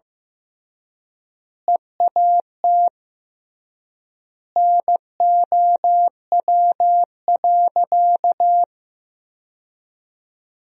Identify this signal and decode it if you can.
Morse code, heard as a keyed tone, about 15 wpm, 700 Hz. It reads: EAT NOW.
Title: 3x Word-Spacing - Sentences from Top 300 Words - 15wpm